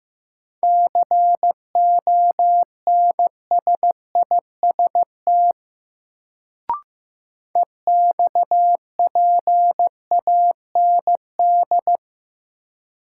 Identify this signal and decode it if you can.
Morse code, heard as a keyed tone, about 15 wpm, 700 Hz. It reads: CONSIST EXPAND
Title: Sentences from Top 1400 Words - New Word Drill Alt - 15wpm